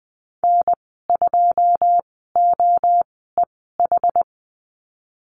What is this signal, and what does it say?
Morse code, heard as a keyed tone, about 20 wpm, spaced as for 15 wpm, 700 Hz. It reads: N2OEH
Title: 082 Lesson 14g - Number 2 - Call Signs 20wpm